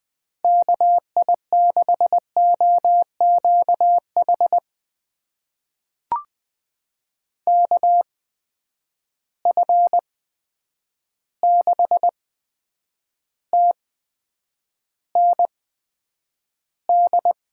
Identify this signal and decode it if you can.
Morse code, heard as a keyed tone, about 20 wpm, spaced as for 9 wpm, 700 Hz. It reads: KI6OQH KF6TND
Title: USA Call Signs - 6 Characters - 8x Character Spacing - 20wpm